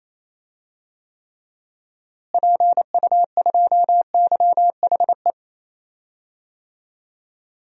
Morse code, heard as a keyed tone, about 28 wpm, 700 Hz. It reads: PU2YHE
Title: DXCC-100 Call Signs - 6 Characters - Repeated 2x - 28wpm